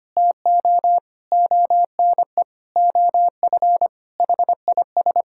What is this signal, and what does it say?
Morse code, heard as a keyed tone, about 25 wpm, 700 Hz. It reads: TO ONE OF HIS
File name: practice / 25wpm / Sentences with 4-Word Phrases - 25wpm